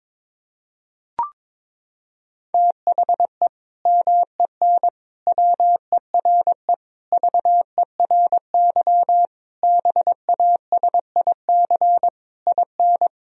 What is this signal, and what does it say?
Morse code, heard as a keyed tone, about 22 wpm, 700 Hz. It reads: THE MEN WERE VERY BASIC IN
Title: Sentences from Top 1000 Words - Review Part 3 - 22wpm